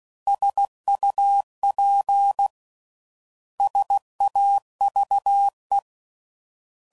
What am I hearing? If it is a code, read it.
Morse code, heard as a keyed tone, about 16 wpm, 800 Hz. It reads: SUP SAVE